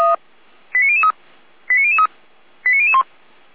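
Five-tone selective call (ZVEI), heard as quick runs of five tones, tones of about 70 ms. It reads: 89003 89003 89001
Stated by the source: ZVEI five-tone selective call received by SDR.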